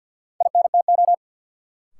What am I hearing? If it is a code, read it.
Morse code, heard as a keyed tone, about 47 wpm, 700 Hz. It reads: INTO
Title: Sentences from Top 600 Words - New Word Drill - 50wpm